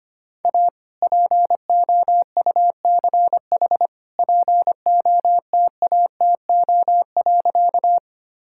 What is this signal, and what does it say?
Morse code, heard as a keyed tone, about 25 wpm, 700 Hz. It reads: A POUCH POTATO.
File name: practice / 25wpm / All-Age Appropriate Jokes - 25wpm